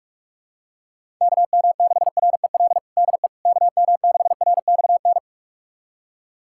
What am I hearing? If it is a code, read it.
Morse code, heard as a keyed tone, about 46 wpm, 700 Hz. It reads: KM6REL DE KM6RXN
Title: USA Callsign DE Callsign - 6 Characters - 45wpm